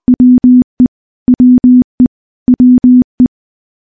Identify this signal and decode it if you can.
Morse code, heard as a keyed tone, about 20 wpm, 250 Hz. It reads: WE WE WE
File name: morse-training-20WPM